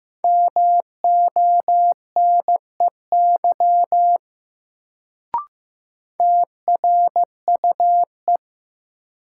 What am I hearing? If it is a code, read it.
Morse code, heard as a keyed tone, about 15 wpm, 700 Hz. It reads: MONEY TRUE
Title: Sentences from Top 400 Words - New Word Drill - 15wpm